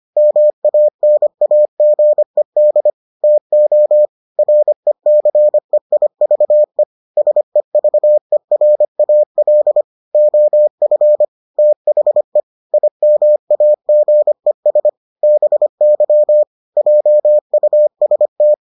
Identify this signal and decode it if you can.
Morse code, heard as a keyed tone, about 25 wpm, 600 Hz. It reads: MANAGED TO RECEIVE SEVERAL OF THE IMAGES BY JUST